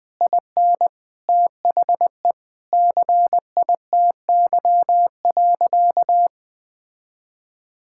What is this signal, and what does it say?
Morse code, heard as a keyed tone, about 20 wpm, 700 Hz. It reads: IN THE CITY.
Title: Sentences with 3-Word Phrases - 20wpm